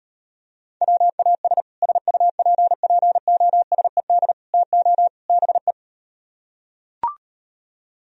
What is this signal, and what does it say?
Morse code, heard as a keyed tone, about 38 wpm, 700 Hz. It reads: WAS SUPPOSED TO BE